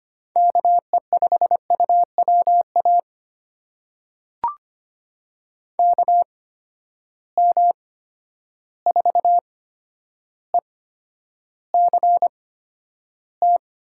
Morse code, heard as a keyed tone, about 25 wpm, spaced as for 11 wpm, 700 Hz. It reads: KE5UWA KM4ECT
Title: USA Call Signs - 6 Characters - 8x Character Spacing - 25wpm